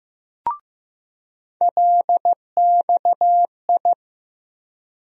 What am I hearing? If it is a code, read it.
Morse code, heard as a keyed tone, about 15 wpm, 700 Hz. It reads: LXI
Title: USA Call Sign Postfixes - 3 Letters 15wpm